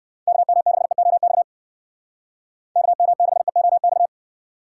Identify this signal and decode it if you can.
Morse code, heard as a keyed tone, about 60 wpm, 700 Hz. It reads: KM6EOX KM6EOX